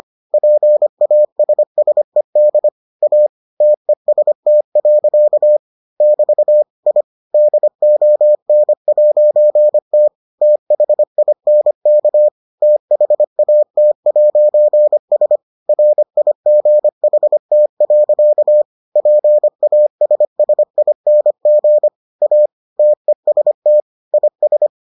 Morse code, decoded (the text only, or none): PASSED A TEST. <BT> I DON'T THINK THAT'S RIGHT. PASSING A TEST IS